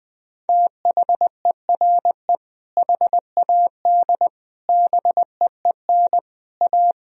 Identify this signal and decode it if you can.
Morse code, heard as a keyed tone, about 21 wpm, 700 Hz. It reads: THERE HAD BEEN A
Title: Sentences from Top 1000 Words - Review Part 2 - 20wpm